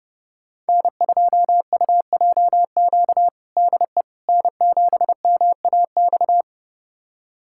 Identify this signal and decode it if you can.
Morse code, heard as a keyed tone, about 30 wpm, 700 Hz. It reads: N2UJQ DE N7MAX
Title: USA Callsign DE Callsign - 5 Characters - 30wpm